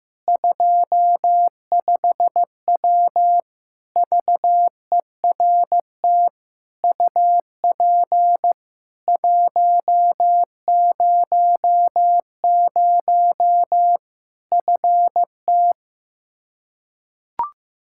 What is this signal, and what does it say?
Morse code, heard as a keyed tone, about 16 wpm, 700 Hz. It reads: 25W VERT UP 100 FT
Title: Practice QSO - Exchange 2 - Info - 15wpm